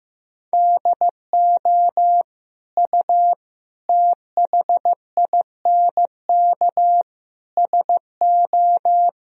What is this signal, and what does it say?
Morse code, heard as a keyed tone, about 15 wpm, 700 Hz. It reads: DO U THINK SO
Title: Sentences from Top 200 Words and Abbreviations 15wpm